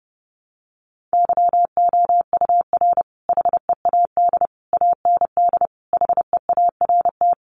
Morse code, heard as a keyed tone, about 30 wpm, 700 Hz. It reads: YOUR HEAD AND HEART
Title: Sentences from Top 1300 Words 30wpm